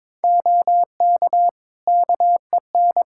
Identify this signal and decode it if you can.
Morse code, heard as a keyed tone, about 21 wpm, 700 Hz. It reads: OK KEN
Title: Practice QSO - Exchange 3 - More Info - 22wpm